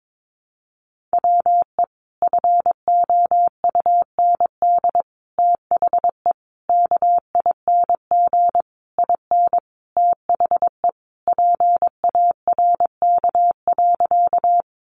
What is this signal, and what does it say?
Morse code, heard as a keyed tone, about 22 wpm, 700 Hz. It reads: WE FOUND THE KING IN THE PARK.
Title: Sentences from Top 1300 Words 22wpm